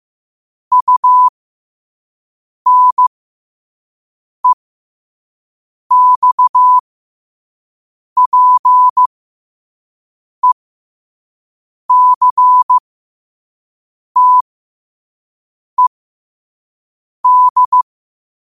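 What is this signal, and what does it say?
Morse code, heard as a keyed tone, about 15 wpm, spaced as for 5 wpm, 1000 Hz. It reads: UNEXPECTED
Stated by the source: Morse code practice at 5 wpm.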